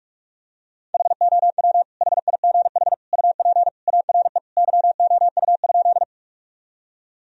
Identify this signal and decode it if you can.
Morse code, heard as a keyed tone, about 46 wpm, 700 Hz. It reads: HOW HIGH UP ARE YOU?
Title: Sentences from Top 1000 Words - Review Part 1 - 45wpm